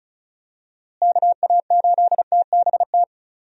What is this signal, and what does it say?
Morse code, heard as a keyed tone, about 36 wpm, 700 Hz. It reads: KA8TBT
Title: USA Call Signs - 6 Characters - Repeated 2x - 35wpm